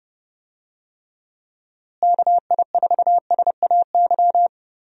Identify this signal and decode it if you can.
Morse code, heard as a keyed tone, about 29 wpm, 700 Hz. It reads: KI4SAY
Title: USA Call Signs - 6 Characters - Repeated 2x - 30wpm